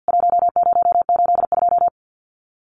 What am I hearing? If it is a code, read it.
Morse code, heard as a keyed tone, about 50 wpm, 700 Hz. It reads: 1082
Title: CWT Exchange 50wpm